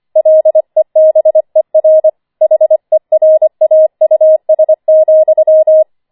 Morse code, heard as a keyed tone, about 25 wpm, 630 Hz. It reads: LEBER HERAUS,
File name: 25wpm